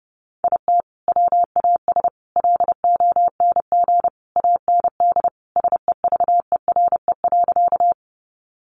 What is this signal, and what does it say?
Morse code, heard as a keyed tone, about 30 wpm, 700 Hz. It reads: IT WAS LONG AND SEVERE.
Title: Sentences from Top 1400 Words 30wpm